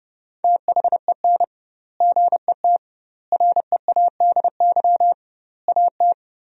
Morse code, heard as a keyed tone, about 30 wpm, 700 Hz. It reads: THEN GET READY AT
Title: 2x Word-Spacing - Sentences from Top 400 Words - 30wpm